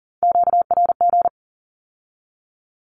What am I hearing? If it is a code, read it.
Morse code, heard as a keyed tone, about 40 wpm, 700 Hz. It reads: QRG
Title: Q-Codes with Spoken Meaning - 40wpm